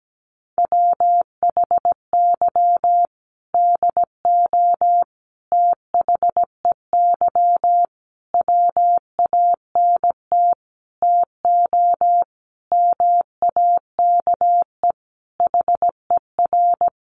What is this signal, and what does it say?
Morse code, heard as a keyed tone, about 17 wpm, 700 Hz. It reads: WHY DO THEY WANT TO MAKE HER
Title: Sentences from Top 400 Words 17wpm